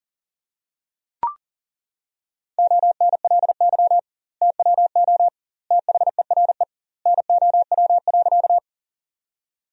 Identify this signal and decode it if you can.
Morse code, heard as a keyed tone, about 40 wpm, 700 Hz. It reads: ONLY TWO THERE NOW.
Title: 2x Word-Spacing - Sentences from Top 200 Words - 40wpm